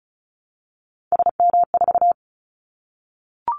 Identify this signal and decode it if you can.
Morse code, heard as a keyed tone, about 36 wpm, 700 Hz. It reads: SM4
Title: Callsign Prefixes for Top 25 DXCC - 35wpm